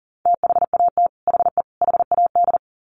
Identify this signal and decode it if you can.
Morse code, heard as a keyed tone, about 40 wpm, 700 Hz. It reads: THAT HE HAD
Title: Sets of 3 Words 40wpm